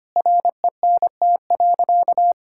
Morse code, heard as a keyed tone, about 25 wpm, 700 Hz. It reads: RENT.
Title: Sentences from Top 1000 Words - Review Part 5 - 25wpm